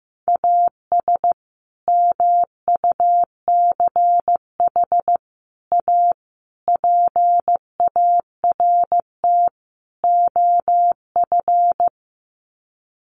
Morse code, heard as a keyed tone, about 15 wpm, 700 Hz. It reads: AS MUCH A PART OF
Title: Sets of 5 Words Encore2 15wpm